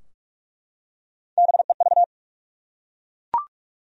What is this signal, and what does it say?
Morse code, heard as a keyed tone, about 46 wpm, 700 Hz. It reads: BEV
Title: USA Common Names - Max 3 Letters - 45wpm